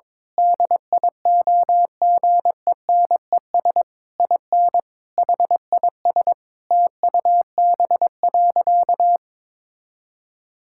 Morse code, heard as a keyed tone, about 22 wpm, 700 Hz. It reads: DIOGENES IN HIS TUB.
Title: Pangram Challenge v2 - 22wpm